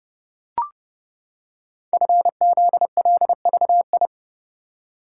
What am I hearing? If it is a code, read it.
Morse code, heard as a keyed tone, about 30 wpm, 700 Hz. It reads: FZLVI
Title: Code Groups - 5 Letters - 30wpm